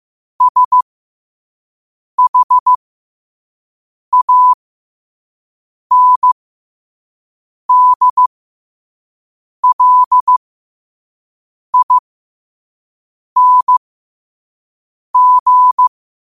Morse code, heard as a keyed tone, about 15 wpm, spaced as for 5 wpm, 1000 Hz. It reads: SHANDLING